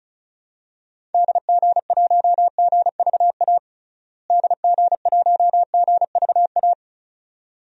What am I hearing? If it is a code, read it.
Morse code, heard as a keyed tone, about 36 wpm, 700 Hz. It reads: DG1GVA DG1GVA
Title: DXCC-100 Call Signs - 6 Characters - Repeated 2x - 35wpm